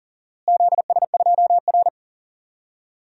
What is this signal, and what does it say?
Morse code, heard as a keyed tone, about 38 wpm, 700 Hz. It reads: ZS2R
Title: DXCC-100 Call Signs - 4 Characters - Repeated 3x - 40wpm